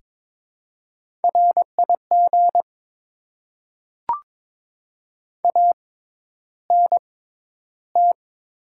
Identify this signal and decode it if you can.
Morse code, heard as a keyed tone, about 22 wpm, spaced as for 11 wpm, 690 Hz. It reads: RIG ANT
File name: Top 20 QSO Elements - 6x Character Spacing - 22wpm